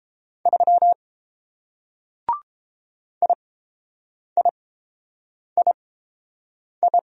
Morse code, heard as a keyed tone, about 29 wpm, spaced as for 7 wpm, 700 Hz. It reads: 3 IIII